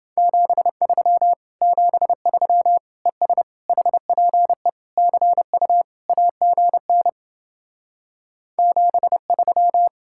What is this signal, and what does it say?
Morse code, heard as a keyed tone, about 28 wpm, 700 Hz. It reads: 73 73 ES HPE CU AGN 73